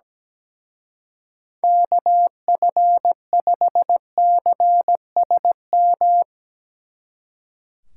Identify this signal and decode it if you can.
Morse code, heard as a keyed tone, about 20 wpm, 700 Hz. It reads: KF5CSM
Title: Speed Racing - USA 2x3 Call Signs 17wpm